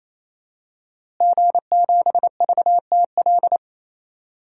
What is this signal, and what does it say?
Morse code, heard as a keyed tone, about 28 wpm, 700 Hz. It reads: G7VTL
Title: DXCC-100 Call Signs - 5 Characters - Repeated 3x - 28wpm